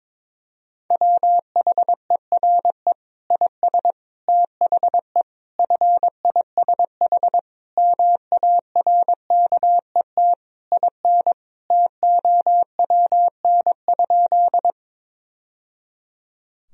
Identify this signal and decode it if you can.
Morse code, heard as a keyed tone, about 22 wpm, 700 Hz. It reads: WHERE IS THE FISH MARKET IN TOWN?
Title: Sentences from Top 1000 Words - Review Part 3 - 22wpm